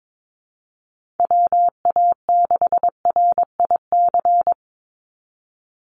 Morse code, heard as a keyed tone, about 22 wpm, 700 Hz. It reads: WA6RIC
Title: NA SOTA Chaser Callsigns 22wpm